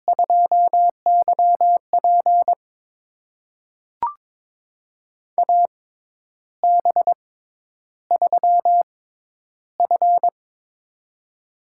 Morse code, heard as a keyed tone, about 21 wpm, spaced as for 10 wpm, 700 Hz. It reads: 2YP AB3F
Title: USA Call Signs - 4 Characters - 6x Character Spacing - 22wpm